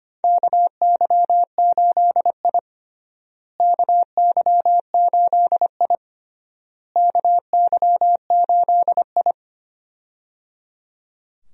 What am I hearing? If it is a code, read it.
Morse code, heard as a keyed tone, about 25 wpm, 700 Hz. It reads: KY8I KY8I KY8I
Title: USA Call Signs - 4 Characters - Repeated 3x - 25wpm